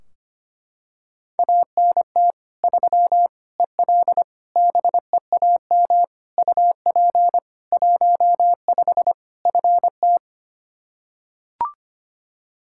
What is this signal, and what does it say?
Morse code, heard as a keyed tone, about 25 wpm, 700 Hz. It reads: ANT 3 EL BEAM UP 15 FT
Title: K1USN SST QSO Practice 25wpm